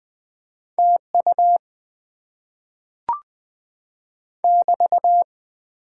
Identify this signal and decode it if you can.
Morse code, heard as a keyed tone, about 20 wpm, 700 Hz. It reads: TU <BT>
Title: Top 90 QSO Elements - 6x Character Spacing - 20wpm